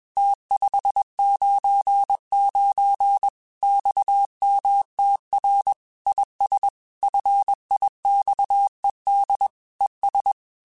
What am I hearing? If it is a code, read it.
Morse code, heard as a keyed tone, about 21 wpm, 800 Hz. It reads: T599 XMTR IS FIXED ES
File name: QSOs / QSOs-mp3s / WA7QOX 1@20